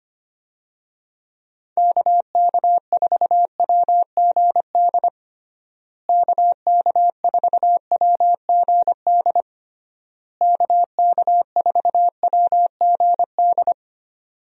Morse code, heard as a keyed tone, about 25 wpm, 700 Hz. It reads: KK4WGD KK4WGD KK4WGD
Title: USA Call Signs - 6 Characters - Repeated 3x - 25wpm